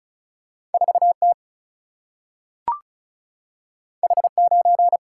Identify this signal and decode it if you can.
Morse code, heard as a keyed tone, about 36 wpm, 700 Hz. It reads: 4T H9
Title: Single-Letter Single-Number 35wpm